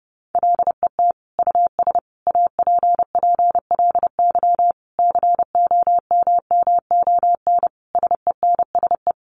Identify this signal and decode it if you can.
Morse code, heard as a keyed tone, about 30 wpm, 700 Hz. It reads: LET US APPLY COMMON SENSE
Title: Sentences from Top 1200 Words 30wpm